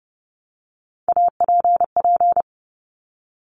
Morse code, heard as a keyed tone, about 30 wpm, 700 Hz. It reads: APP
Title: Top Three Letter Combinations in Words 30wpm